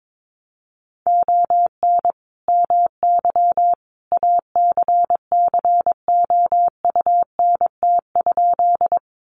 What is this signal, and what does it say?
Morse code, heard as a keyed tone, about 22 wpm, 700 Hz. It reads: ON MY ACCOUNT?
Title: Sentences from Top 900 Words 22wpm